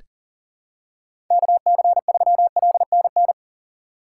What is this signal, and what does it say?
Morse code, heard as a keyed tone, about 40 wpm, 700 Hz. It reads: KC3LNN
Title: USA Call Signs - 6 Characters - 2x Character Spacing - 40wpm